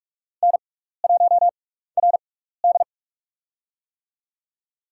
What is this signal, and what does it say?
Morse code, heard as a keyed tone, about 46 wpm, spaced as for 15 wpm, 700 Hz. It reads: N1RD
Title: USA Call Signs - 6x Character Spacing - 45wpm